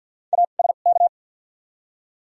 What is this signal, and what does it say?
Morse code, heard as a keyed tone, about 53 wpm, 700 Hz. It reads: A S K
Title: 2x Character Spacing - Top 300 Words - 50wpm